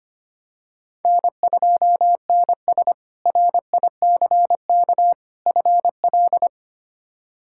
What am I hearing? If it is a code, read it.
Morse code, heard as a keyed tone, about 25 wpm, 700 Hz. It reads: N2NS RICK FL
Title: K1USN SST QSO Practice - 25wpm